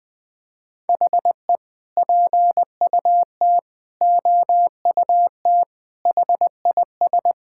Morse code, heard as a keyed tone, about 20 wpm, 700 Hz. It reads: HE PUT OUT HIS
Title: Sentences from Top 1000 Words - Review Part 5 - 20wpm